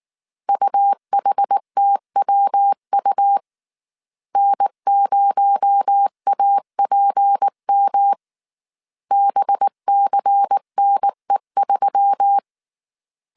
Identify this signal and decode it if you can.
Morse code, heard as a keyed tone, about 19 wpm, 800 Hz. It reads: UHTWU N0APM BCNE3
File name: CW / DET / 1 6 1500C S A 18wpm 2255